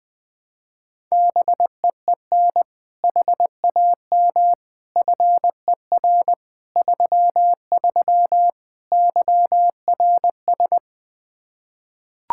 Morse code, heard as a keyed tone, about 21 wpm, 700 Hz. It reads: BEEN HAM FER 33 YRS
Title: Practice QSO - Exchange 3 - More Info - 20wpm